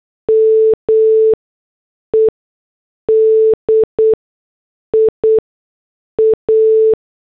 Morse code, heard as a keyed tone, about 8 wpm, spaced as for 6 wpm, 440 Hz. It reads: MEDIA